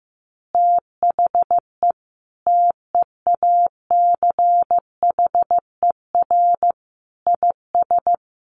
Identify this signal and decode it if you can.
Morse code, heard as a keyed tone, about 14 wpm, 700 Hz. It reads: THE TEACHER IS
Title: Easy Phrases 15wpm